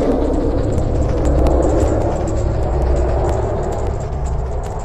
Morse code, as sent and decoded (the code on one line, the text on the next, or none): none